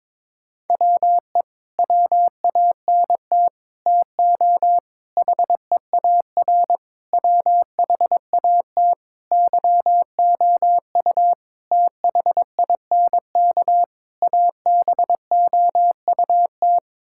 Morse code, heard as a keyed tone, about 22 wpm, 700 Hz. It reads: WE WANT TO HEAR WHAT YOU THINK ABOUT
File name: practice / 22wpm / Sets of 8 Words - 22wpm